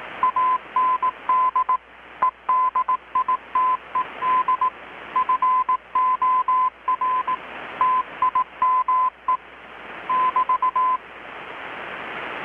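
Morse code, heard as a keyed tone, about 18 wpm, 1020 Hz. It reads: AND EDITED FOR TIME <BT>
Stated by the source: Morse from KPH's Night of Nights XX maritime radiotelegraphy event, received in Chiapas, Mexico about 3,800 km away on a Kenwood R-600 with a grounded 10 m long-wire antenna.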